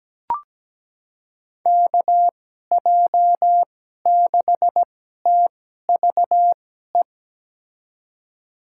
Morse code, heard as a keyed tone, about 17 wpm, spaced as for 12 wpm, 700 Hz. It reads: KJ6TVE
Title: USA Call Signs - 2x Character Spacing - 17wpm